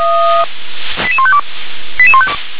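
Five-tone selective call (ZVEI), heard as quick runs of five tones, tones of about 70 ms. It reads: 88152
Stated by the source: ZVEI five-tone selective call received on an SDR.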